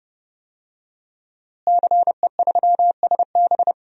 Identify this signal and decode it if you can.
Morse code, heard as a keyed tone, about 31 wpm, 700 Hz. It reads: CE3SB
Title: DXCC-100 Call Signs - 5 Characters - Repeated 2x - 30wpm